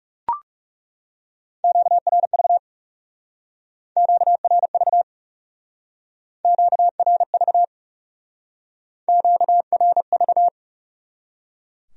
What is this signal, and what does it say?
Morse code, heard as a keyed tone, about 38 wpm, 700 Hz. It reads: QRV QRV QRV QRV